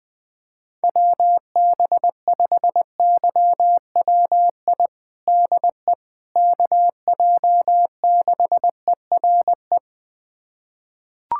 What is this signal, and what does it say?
Morse code, heard as a keyed tone, about 21 wpm, 700 Hz. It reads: WB5YWI DE KJ6ERE
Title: USA Callsign DE Callsign - 6 Characters - 20wpm